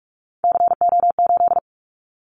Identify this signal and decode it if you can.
Morse code, heard as a keyed tone, about 44 wpm, 700 Hz. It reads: CO8
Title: Callsign Prefixes for Top 100 DXCC - 45wpm